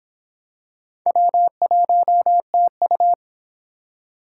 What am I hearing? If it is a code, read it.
Morse code, heard as a keyed tone, about 26 wpm, 700 Hz. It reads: W1TU